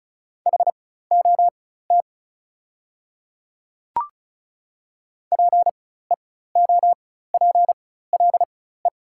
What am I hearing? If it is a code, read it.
Morse code, heard as a keyed tone, about 36 wpm, spaced as for 16 wpm, 690 Hz. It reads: HOT PEOPLE
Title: 4x Character Spacing - Top 200 Words - 35wpm